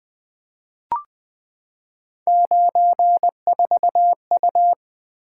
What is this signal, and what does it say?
Morse code, heard as a keyed tone, about 20 wpm, 700 Hz. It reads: 94U
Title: Three Letter-Number Combinations - 20wpm